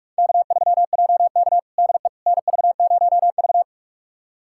Morse code, heard as a keyed tone, about 47 wpm, 700 Hz. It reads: K3JK DE NV0V
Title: USA Callsign DE Callsign - 4 Characters - 45wpm